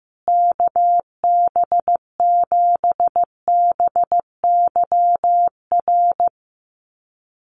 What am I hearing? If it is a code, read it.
Morse code, heard as a keyed tone, about 15 wpm, 700 Hz. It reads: KB7BYR
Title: USA Call Signs Repeated 2x 15wpm